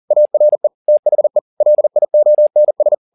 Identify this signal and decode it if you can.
Morse code, heard as a keyed tone, about 40 wpm, 600 Hz. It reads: ARE THE LIONS